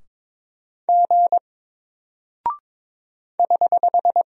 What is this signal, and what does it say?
Morse code, heard as a keyed tone, about 22 wpm, 700 Hz. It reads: G <HH>
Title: Single Character (Letter:Number:Punctuation:Prosign) Alt 22wpm